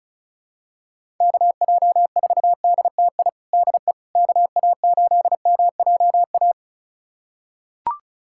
Morse code, heard as a keyed tone, about 35 wpm, 700 Hz. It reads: KJ4DTI DE KA8MJA
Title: USA Callsign DE Callsign 35wpm